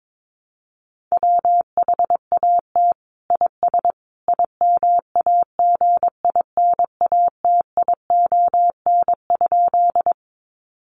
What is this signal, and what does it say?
Morse code, heard as a keyed tone, about 22 wpm, 700 Hz. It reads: WHAT IS IMAGINATION?
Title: Sentences from Top 1300 Words 22wpm